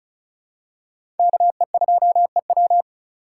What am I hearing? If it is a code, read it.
Morse code, heard as a keyed tone, about 36 wpm, 700 Hz. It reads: KE2EW
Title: USA Call Signs - 5 Characters - 2x Character Spacing - 35wpm